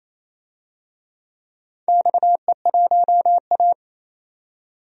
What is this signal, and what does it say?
Morse code, heard as a keyed tone, about 28 wpm, 700 Hz. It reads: XE1A
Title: DXCC-100 Call Signs - 4 Characters - 28wpm